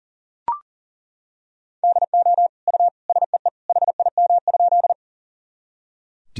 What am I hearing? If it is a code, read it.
Morse code, heard as a keyed tone, about 40 wpm, 700 Hz. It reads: DO U SEE HIM?
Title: Sentences from Top 100 Words and Abbreviations 40wpm